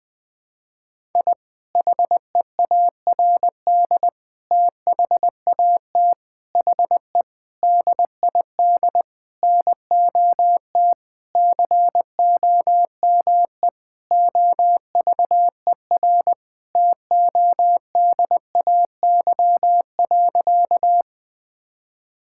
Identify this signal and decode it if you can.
Morse code, heard as a keyed tone, about 21 wpm, 700 Hz. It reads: I HEARD THAT HE DID NOT COME OVER TODAY.
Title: Sentences with 4-Word Phrases - 20wpm